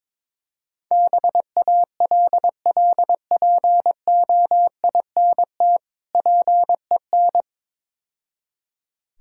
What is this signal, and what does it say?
Morse code, heard as a keyed tone, about 23 wpm, 700 Hz. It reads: BALLPOINT PEN
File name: Easy Words - Speed-Racing with Hints - 22wpm